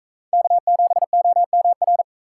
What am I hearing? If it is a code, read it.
Morse code, heard as a keyed tone, about 40 wpm, 700 Hz. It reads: K7OMR